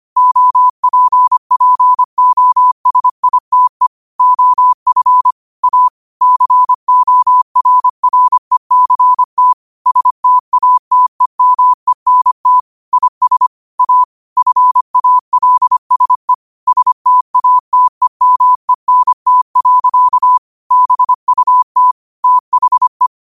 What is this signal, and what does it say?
Morse code, heard as a keyed tone, about 25 wpm, 1000 Hz. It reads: OPPOSITE OF A CORRECT STATEMENT IS A FALSE STATEMENT. BUT THE
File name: qotdfiles / 25wpm